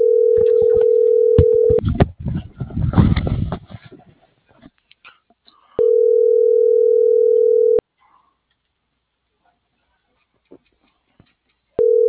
Ringback tone (North America), 440 + 480 Hz.